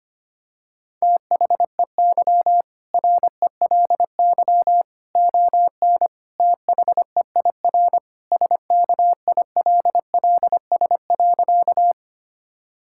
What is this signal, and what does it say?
Morse code, heard as a keyed tone, about 25 wpm, 700 Hz. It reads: THEY RELY ON THEIR SKILLS.
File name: practice / 25wpm / Sentences with 2-Word Phrases - 25wpm